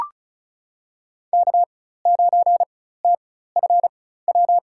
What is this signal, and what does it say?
Morse code, heard as a keyed tone, about 34 wpm, spaced as for 16 wpm, 700 Hz. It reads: K9TFW
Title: USA Call Signs - 5 Characters - 4x Character Spacing - 35wpm